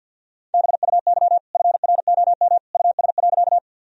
Morse code, heard as a keyed tone, about 50 wpm, 700 Hz. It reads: BUY FROM US.
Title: Sentences from Top 1000 Words - Review Part 4 - 50wpm